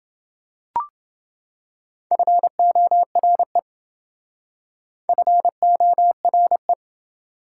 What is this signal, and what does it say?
Morse code, heard as a keyed tone, about 27 wpm, 700 Hz. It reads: FORE FORE